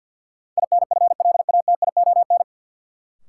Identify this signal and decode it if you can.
Morse code, heard as a keyed tone, about 52 wpm, 700 Hz. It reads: INFLATION
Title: Sentences from Top 1100 Words - New Word Drill - 50wpm